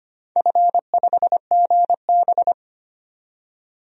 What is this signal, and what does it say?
Morse code, heard as a keyed tone, about 25 wpm, 700 Hz. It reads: F5GB
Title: DXCC-100 Call Signs - 4 Characters - 25wpm